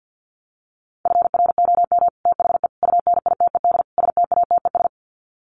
Morse code, heard as a keyed tone, about 50 wpm, 700 Hz. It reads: FROM THE UNITED STATES
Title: Sets of 4 Words 50wpm